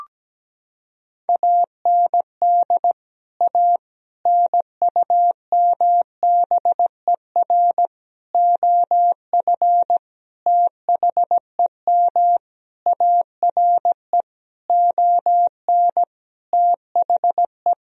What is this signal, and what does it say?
Morse code, heard as a keyed tone, about 17 wpm, 700 Hz. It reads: AND A NUMBER OF THEM ARE ON THE
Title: Sentences from Top 1000 Words - Review Part 1 - 17wpm